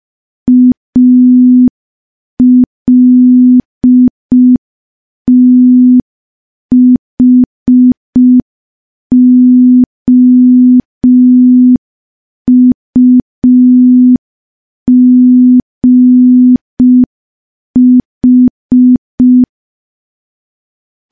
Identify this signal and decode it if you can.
Morse code, heard as a keyed tone, about 5 wpm, 250 Hz. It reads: ALTHOUGH
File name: morse / morse-training-05WPM